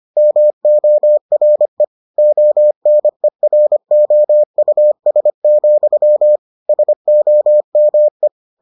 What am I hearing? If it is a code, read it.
Morse code, heard as a keyed tone, about 25 wpm, 600 Hz. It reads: MORE ONEROUS, SOME